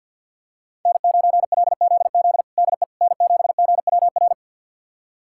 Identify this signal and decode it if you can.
Morse code, heard as a keyed tone, about 51 wpm, 700 Hz. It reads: N9LZZ DE N7GWR
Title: USA Callsign DE Callsign - 5 Characters - 50wpm